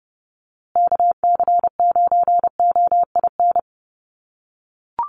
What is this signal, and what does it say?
Morse code, heard as a keyed tone, about 30 wpm, 700 Hz. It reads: KC9OIN
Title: USA Call Signs Repeated 2x 30wpm